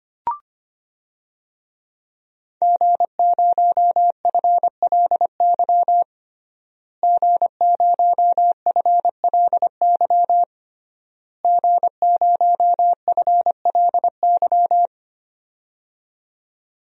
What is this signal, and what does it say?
Morse code, heard as a keyed tone, about 25 wpm, 700 Hz. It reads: G0FLY G0FLY G0FLY
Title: DXCC-100 Call Signs - 5 Characters - Repeated 3x - 25wpm